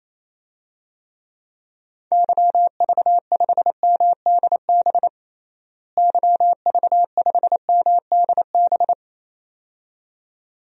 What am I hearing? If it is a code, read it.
Morse code, heard as a keyed tone, about 28 wpm, 700 Hz. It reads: YV5MDB YV5MDB
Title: DXCC-100 Call Signs - 6 Characters - Repeated 2x - 28wpm